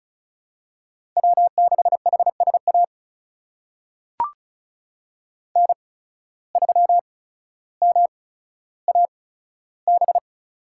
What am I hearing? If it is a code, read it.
Morse code, heard as a keyed tone, about 34 wpm, spaced as for 15 wpm, 700 Hz. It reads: W6HSA N3MAB
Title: USA Call Signs - 5 Characters - 8x Character Spacing - 35wpm